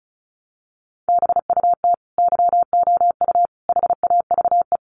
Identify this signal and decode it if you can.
Morse code, heard as a keyed tone, about 34 wpm, 700 Hz. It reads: BUT YOU HAVE
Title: Sentences from Top 1100 Words 35wpm